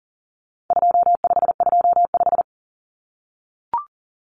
Morse code, heard as a keyed tone, about 40 wpm, 700 Hz. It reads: 2525
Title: Common Cut Numbers - 4-Digit 40wpm